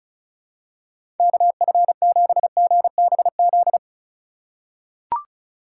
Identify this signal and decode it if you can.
Morse code, heard as a keyed tone, about 34 wpm, 700 Hz. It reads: KF7GBZ
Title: USA Call Signs - 6 Characters - 4x Character Spacing - 35wpm